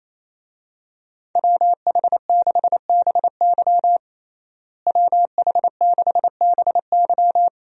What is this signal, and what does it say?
Morse code, heard as a keyed tone, about 28 wpm, 700 Hz. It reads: WH6BY WH6BY
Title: USA Call Signs - 5 Characters - Repeated 2x - 28wpm